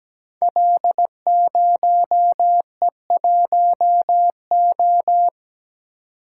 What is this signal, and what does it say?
Morse code, heard as a keyed tone, about 17 wpm, 700 Hz. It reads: L0E1O
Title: Code Groups - 5 Letter:Numbers - 17wpm